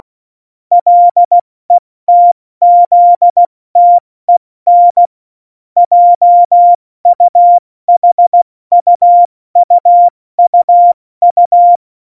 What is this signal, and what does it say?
Morse code, heard as a keyed tone, about 16 wpm, 700 Hz. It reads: LETZTEN JUHUUUU
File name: Morsekurs / Witze / Witz 11 August 2020, 70 F80